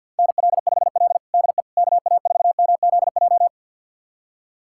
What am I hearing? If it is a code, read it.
Morse code, heard as a keyed tone, about 49 wpm, 700 Hz. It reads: NL5L DE KA4MZJ
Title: USA Callsign DE Callsign 50wpm